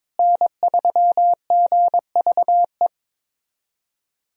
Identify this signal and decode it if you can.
Morse code, heard as a keyed tone, about 22 wpm, 700 Hz. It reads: N3GVE
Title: USA Call Signs - 5 Characters - Repeated 3x - 22wpm